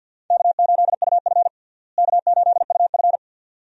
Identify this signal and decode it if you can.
Morse code, heard as a keyed tone, about 50 wpm, 700 Hz. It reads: K8UF K8UF
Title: USA Call Signs - 4 Characters - Repeated 2x - 50wpm